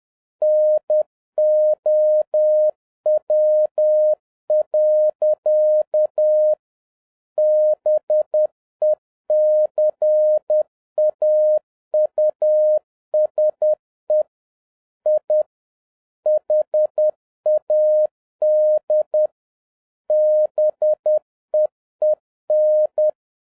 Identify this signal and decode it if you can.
Morse code, heard as a keyed tone, about 10 wpm, 610 Hz. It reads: NOW. BECAUSE I HAD BEEN
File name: 2011-07-mp3 / n41 EXCITING QRP CW OPERATION 10wpm